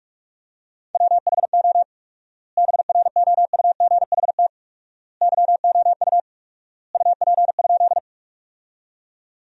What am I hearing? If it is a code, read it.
Morse code, heard as a keyed tone, about 45 wpm, 700 Hz. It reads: WHO BROUGHT YOU UP?
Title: 4x Word-Spacing - Sentences from Top 500 Words - 45wpm